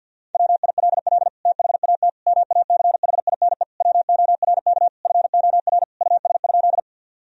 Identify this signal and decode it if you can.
Morse code, heard as a keyed tone, about 50 wpm, 700 Hz. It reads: WILL THAT MACHINE WORK FOR US?
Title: Sentences from Top 1000 Words - Review Part 2 - 50wpm